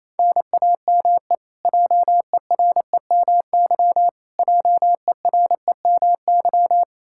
Morse code, heard as a keyed tone, about 28 wpm, 700 Hz. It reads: NAME JEREMY JEREMY
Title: Practice QSO - Exchange 1 - Beginning - 28wpm